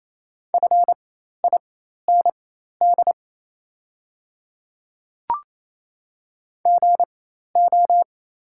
Morse code, heard as a keyed tone, about 27 wpm, spaced as for 13 wpm, 700 Hz. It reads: FIND GO
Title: 4x Character Spacing - Top 100 Words - 28wpm